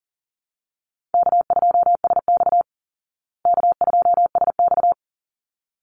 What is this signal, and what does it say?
Morse code, heard as a keyed tone, about 40 wpm, 700 Hz. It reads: K2SX K2SX
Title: Call Signs Repeated 3x - 40wpm